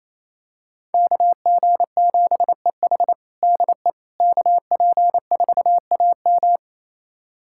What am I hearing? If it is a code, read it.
Morse code, heard as a keyed tone, about 28 wpm, 700 Hz. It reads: KG7EH DE KP4AM
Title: USA Callsign DE Callsign - 5 Characters - 28wpm